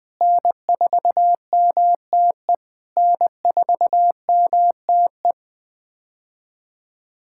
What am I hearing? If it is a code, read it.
Morse code, heard as a keyed tone, about 20 wpm, 700 Hz. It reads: N4MTE N4MTE
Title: POTA QSO Practice - 20wpm